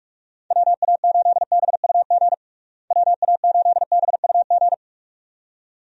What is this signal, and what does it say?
Morse code, heard as a keyed tone, about 46 wpm, 700 Hz. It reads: WA8BUG WA8BUG
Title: USA Call Signs - 6 Characters - Repeated 2x - 45wpm